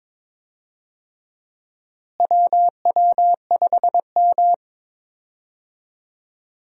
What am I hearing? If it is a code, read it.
Morse code, heard as a keyed tone, about 22 wpm, 700 Hz. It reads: WW5M
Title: USA Call Signs - 4 Characters - Repeated 2x - 22wpm